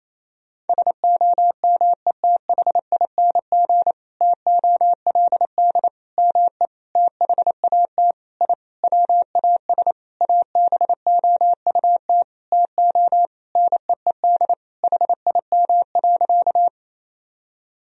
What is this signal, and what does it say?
Morse code, heard as a keyed tone, about 28 wpm, 700 Hz. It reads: SOMETHING TOLD ME THAT I WAS ABOUT TO NEED HIM.
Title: Sentences from Top 1000 Words - Review Part 1 - 28wpm